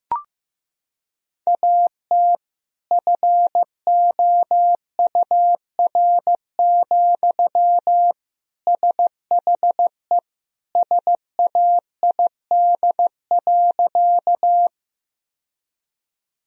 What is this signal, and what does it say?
Morse code, heard as a keyed tone, about 15 wpm, 700 Hz. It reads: AT FOUR, SHE SAID.
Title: Sentences from Top 1000 Words - Review Part 1 - 15wpm